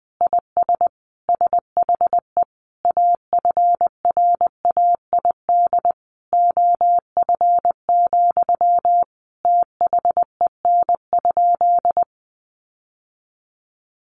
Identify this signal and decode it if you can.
Morse code, heard as a keyed tone, about 19 wpm, 700 Hz. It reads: IS SHE AFRAID OF, THEN?
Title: Sentences from Top 1500 Words 20wpm